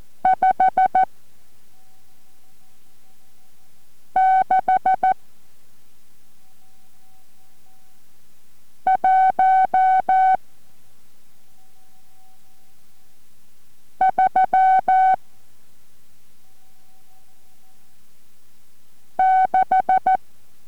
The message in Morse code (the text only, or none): 56136